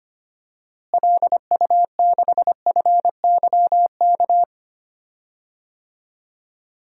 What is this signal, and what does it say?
Morse code, heard as a keyed tone, about 25 wpm, 700 Hz. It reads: LU6FYK
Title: DXCC-100 Call Signs - 6 Characters - Repeated 3x - 25wpm